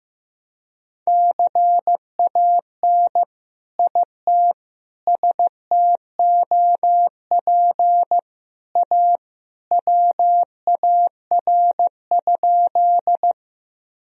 Morse code, heard as a keyed tone, about 16 wpm, 700 Hz. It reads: CAN IT STOP A WAR?
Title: Sentences from Top 1000 Words - Review Part 2 - 15wpm